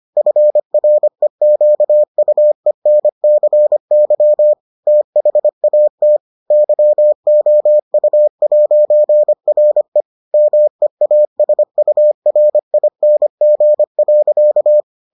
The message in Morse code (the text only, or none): FREQUENCY THAT YOU'RE MEASURING.